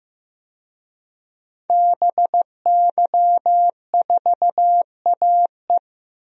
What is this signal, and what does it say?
Morse code, heard as a keyed tone, about 15 wpm, 700 Hz. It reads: BY4AE
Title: DXCC-100 Call Signs - 5 Characters - Repeated 2x - 15wpm